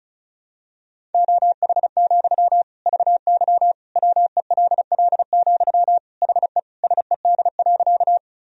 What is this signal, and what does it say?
Morse code, heard as a keyed tone, about 34 wpm, 700 Hz. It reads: OH, VY WELL, HE SED.
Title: Sentences from Top 500 Words and Abbreviations 35wpm